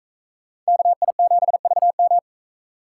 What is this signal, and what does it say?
Morse code, heard as a keyed tone, about 43 wpm, 700 Hz. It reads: KI7VM